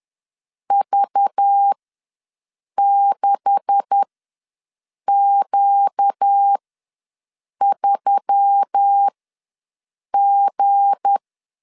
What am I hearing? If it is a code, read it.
Morse code, heard as a keyed tone, about 10 wpm, spaced as for 6 wpm, 800 Hz. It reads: V6Q3G
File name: LEZIONI / 9 286 ws 10wpm 4032